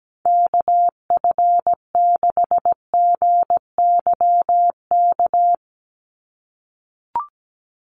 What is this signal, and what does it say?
Morse code, heard as a keyed tone, about 17 wpm, 700 Hz. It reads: KF6GYK
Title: USA Call Signs 17wpm